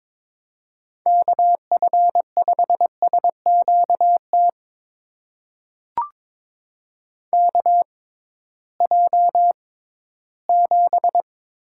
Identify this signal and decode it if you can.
Morse code, heard as a keyed tone, about 22 wpm, spaced as for 14 wpm, 700 Hz. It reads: KF5SQT KJ7
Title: USA Call Signs - 6 Characters - 6x Character Spacing - 22wpm